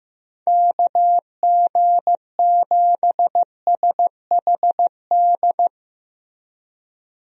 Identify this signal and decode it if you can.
Morse code, heard as a keyed tone, about 15 wpm, 700 Hz. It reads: KG7SHD
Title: USA Call Signs - 6 Characters - Repeated 2x - 15wpm